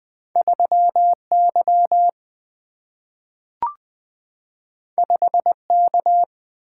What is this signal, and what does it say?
Morse code, heard as a keyed tone, about 20 wpm, 700 Hz. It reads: 3 Y 5 K
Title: Single-Letter Single-Number 20wpm